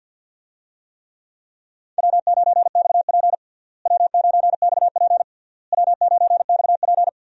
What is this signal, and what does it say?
Morse code, heard as a keyed tone, about 48 wpm, 700 Hz. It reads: W9XP W9XP W9XP
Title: USA Call Signs - 4 Characters - Repeated 3x - 50wpm